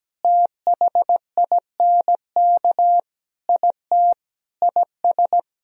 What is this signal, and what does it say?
Morse code, heard as a keyed tone, about 17 wpm, 700 Hz. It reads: THINK IT IS
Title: Sentences from Top 1000 Words - Review Part 4 - 17wpm